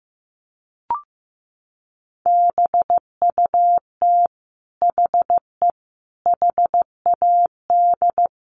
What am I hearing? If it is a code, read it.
Morse code, heard as a keyed tone, about 15 wpm, 700 Hz. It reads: BUT HE HAD
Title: Sentences from Top 1500 Words 15wpm